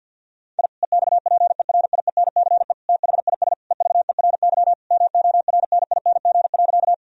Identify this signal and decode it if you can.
Morse code, heard as a keyed tone, about 52 wpm, 700 Hz. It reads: I EXPERIENCE THIS EVERY MORNING.
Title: Sentences from Top 1000 Words - Review Part 3 - 50wpm